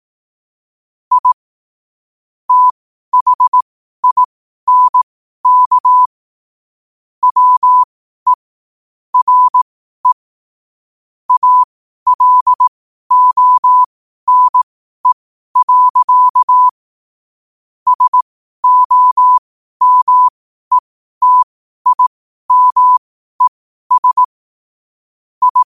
Morse code, heard as a keyed tone, about 18 wpm, spaced as for 12 wpm, 1000 Hz. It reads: I THINK WE RE ALONE. SOMETIMES I
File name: qotdfiles / 12wpm